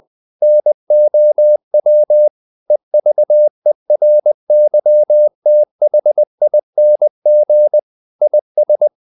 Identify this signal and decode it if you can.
Morse code, heard as a keyed tone, about 20 wpm, 600 Hz. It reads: NOW EVERYTHING IS
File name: the old man and the sea / the old man and the sea part 1 20wpm